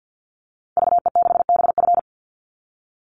Morse code, heard as a keyed tone, about 50 wpm, 700 Hz. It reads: VE6BF
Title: Call Signs Repeated 2x - 50wpm